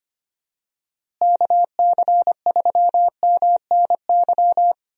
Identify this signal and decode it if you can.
Morse code, heard as a keyed tone, about 25 wpm, 700 Hz. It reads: KC3MNY
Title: USA Call Signs - 6 Characters - Repeated 2x - 25wpm